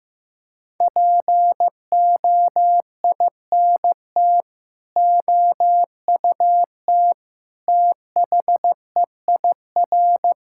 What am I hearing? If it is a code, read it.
Morse code, heard as a keyed tone, about 15 wpm, 700 Hz. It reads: POINT OUT THEIR
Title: Sentences from Top 1000 Words - Review Part 6 - 15wpm